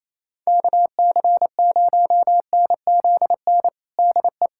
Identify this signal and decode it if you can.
Morse code, heard as a keyed tone, about 28 wpm, 700 Hz. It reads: KC0NZN DE
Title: USA Callsign DE Callsign 28wpm